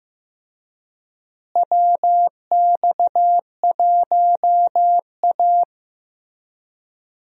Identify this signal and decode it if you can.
Morse code, heard as a keyed tone, about 15 wpm, 700 Hz. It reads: WX1A
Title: USA Call Signs - 4 Characters - Repeated 2x - 15wpm